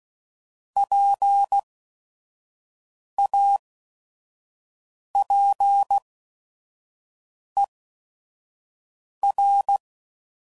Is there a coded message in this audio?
Yes: Morse code from 0.8 to 9.8 s, heard as a keyed tone, about 16 wpm, spaced as for 5 wpm, 800 Hz. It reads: PAPER